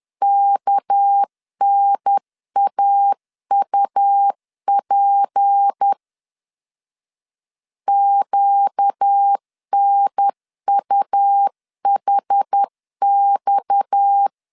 Morse code, heard as a keyed tone, about 10 wpm, 800 Hz. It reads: KNAUP QNUHX
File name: CW / DET / 1 6 1500C S C 10wpm 4252